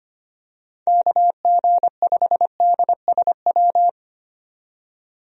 Morse code, heard as a keyed tone, about 25 wpm, 700 Hz. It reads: KG5DSW
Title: USA Call Signs - 6 Characters - 25wpm